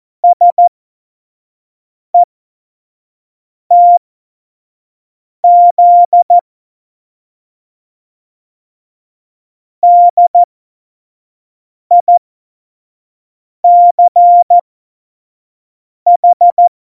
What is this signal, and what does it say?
Morse code, heard as a keyed tone, about 14 wpm, spaced as for 5 wpm, 700 Hz. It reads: SETZ DICH
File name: Morsekurs / Witze / Witz 18 August 2020, 25 F70